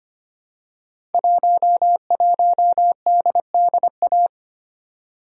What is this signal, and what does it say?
Morse code, heard as a keyed tone, about 25 wpm, 700 Hz. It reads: 11DDA
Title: Code Groups - 5 Letter:Numbers - 25wpm